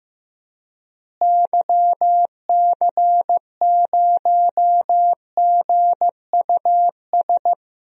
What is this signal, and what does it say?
Morse code, heard as a keyed tone, about 16 wpm, 700 Hz. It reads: YC0GUS
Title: DXCC-100 Call Signs - 6 Characters - Repeated 3x - 15wpm